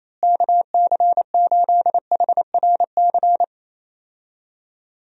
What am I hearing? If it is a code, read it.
Morse code, heard as a keyed tone, about 28 wpm, 700 Hz. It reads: KC8HRC
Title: USA Call Signs - 6 Characters - Repeated 3x - 28wpm